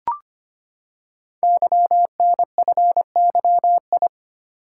Morse code, heard as a keyed tone, about 25 wpm, 700 Hz. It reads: YNFYI
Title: Code Groups - 5 Letters - 25wpm